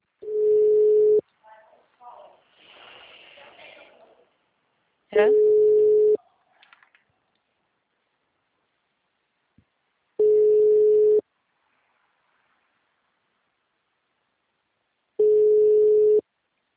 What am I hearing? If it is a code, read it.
Ringback tone (Europe, 425 Hz), with speech.
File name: out- -2031- - - .11785